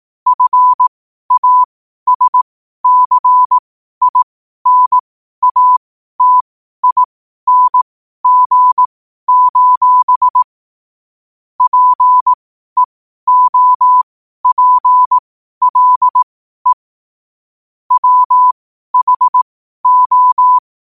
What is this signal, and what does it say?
Morse code, heard as a keyed tone, about 18 wpm, spaced as for 12 wpm, 1000 Hz. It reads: FASCINATING: PEOPLE WHO
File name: qotdfiles / 12wpm